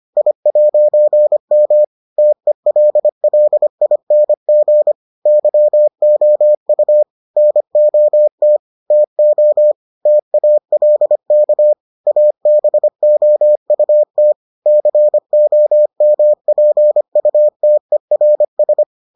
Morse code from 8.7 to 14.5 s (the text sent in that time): TO TALK ABOUT